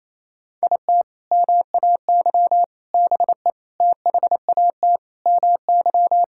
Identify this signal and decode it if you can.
Morse code, heard as a keyed tone, about 28 wpm, 700 Hz. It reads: IT MAY BE THAT MY
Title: Sentences from Top 1600 Words 28wpm